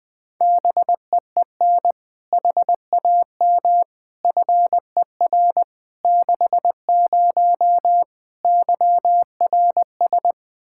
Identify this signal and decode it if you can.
Morse code, heard as a keyed tone, about 21 wpm, 700 Hz. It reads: BEEN HAM FER 60 YRS
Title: K1USN SST QSO Practice 20wpm